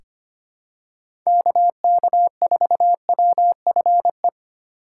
Morse code, heard as a keyed tone, about 25 wpm, 700 Hz. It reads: KK4WFE
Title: USA Call Signs - 6 Characters - 2x Character Spacing - 25wpm